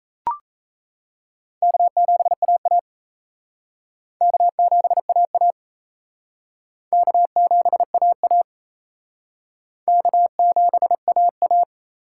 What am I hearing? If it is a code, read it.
Morse code, heard as a keyed tone, about 35 wpm, 700 Hz. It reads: K7AA K7AA K7AA K7AA